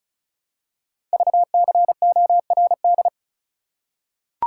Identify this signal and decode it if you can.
Morse code, heard as a keyed tone, about 36 wpm, 700 Hz. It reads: VCORD
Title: Code Groups - 5 Letters - 35wpm